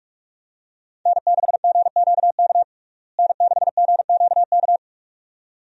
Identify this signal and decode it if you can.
Morse code, heard as a keyed tone, about 46 wpm, 700 Hz. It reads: N6GQK N6GQK
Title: USA Call Signs - 5 Characters - Repeated 2x - 45wpm